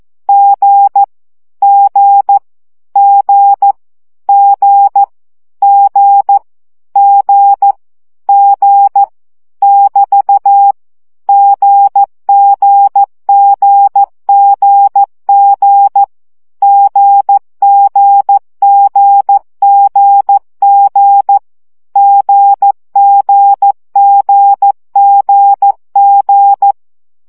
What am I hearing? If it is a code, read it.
Morse code, heard as a keyed tone, about 14 wpm, 800 Hz. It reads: G G G G G G G <BT> GGGGG GGGGG GGGGG
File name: CW gyakorlas g 12 baud